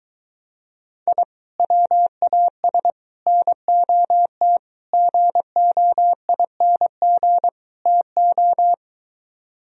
Morse code, heard as a keyed tone, about 23 wpm, 700 Hz. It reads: I WAS NOT GOING TO